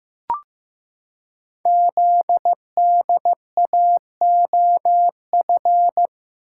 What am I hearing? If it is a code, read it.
Morse code, heard as a keyed tone, about 15 wpm, 700 Hz. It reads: ZDAOF
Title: Code Groups - 5 Letter:Numbers - 15wpm